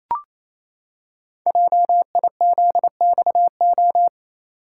Morse code, heard as a keyed tone, about 28 wpm, 700 Hz. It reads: JIZXO
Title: Code Groups - 5 Letters - 28wpm